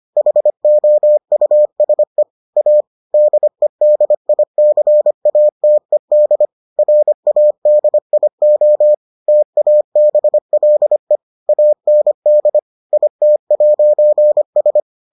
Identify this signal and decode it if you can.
Morse code, heard as a keyed tone, about 25 wpm, 600 Hz. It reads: HOUSE A DEDICATED RADIO TABLE AND IT'S